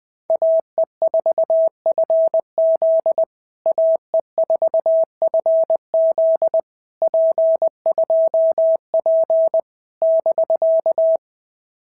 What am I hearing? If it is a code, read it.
Morse code, heard as a keyed tone, about 19 wpm, 650 Hz. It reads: AE4FZ AE4FZ P2P <BK>
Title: POTA QSO Practice - 20wpm